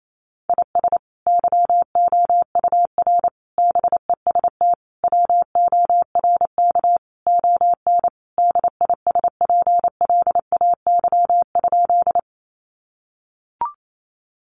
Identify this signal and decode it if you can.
Morse code, heard as a keyed tone, about 28 wpm, 700 Hz. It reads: IS YOUR BEST WORK ON DISPLAY?
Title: Sentences from Top 1200 Words 28wpm